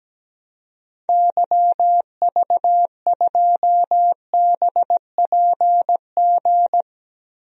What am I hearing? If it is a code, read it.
Morse code, heard as a keyed tone, about 17 wpm, 700 Hz. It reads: YV2BPG
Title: DXCC-100 Call Signs - 6 Characters - Repeated 2x - 17wpm